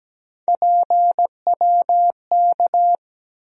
Morse code, heard as a keyed tone, about 17 wpm, 700 Hz. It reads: PWK
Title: USA Call Sign Postfixes - 3 Letters 17wpm